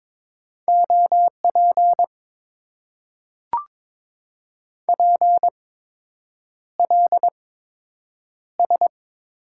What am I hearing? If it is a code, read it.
Morse code, heard as a keyed tone, about 22 wpm, spaced as for 8 wpm, 690 Hz. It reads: OP PLS
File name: Top 90 QSO Elements - 8x Character Spacing - 22wpm